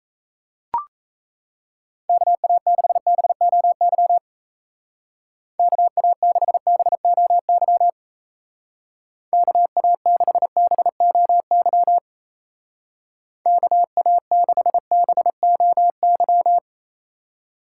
Morse code, heard as a keyed tone, about 34 wpm, 700 Hz. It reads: KA6BOY KA6BOY KA6BOY KA6BOY
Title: Speed Racing - USA Call Signs 28wpm